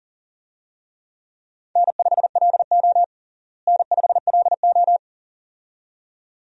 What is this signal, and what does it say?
Morse code, heard as a keyed tone, about 40 wpm, 700 Hz. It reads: N5LO N5LO
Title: USA Call Signs - 4 Characters - Repeated 2x - 40wpm